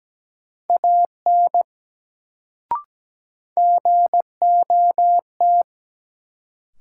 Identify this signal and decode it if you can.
Morse code, heard as a keyed tone, about 17 wpm, 700 Hz. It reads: AN GOT
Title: Sentences from Top 300 Words - New Word Drill - 17wpm